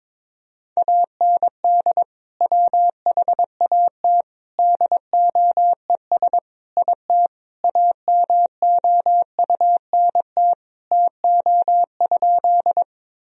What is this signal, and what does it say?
Morse code, heard as a keyed tone, about 22 wpm, 700 Hz. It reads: AND WHAT DOES IT AMOUNT TO?
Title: Sentences from Top 1000 Words - Review Part 4 - 22wpm